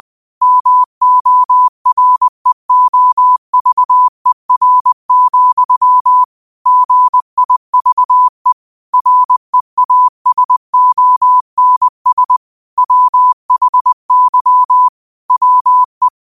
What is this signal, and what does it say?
Morse code, heard as a keyed tone, about 20 wpm, 1000 Hz. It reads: MOREOVER, GIVE REASONS WHY WE